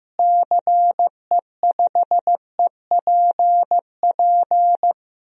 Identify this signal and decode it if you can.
Morse code, heard as a keyed tone, about 15 wpm, 700 Hz. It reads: CE5EPP
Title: DXCC-100 Call Signs - 6 Characters - Repeated 3x - 15wpm